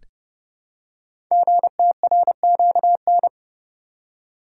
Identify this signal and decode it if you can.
Morse code, heard as a keyed tone, about 30 wpm, 700 Hz. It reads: GTRQN
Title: Code Groups - 5 Letters - 30wpm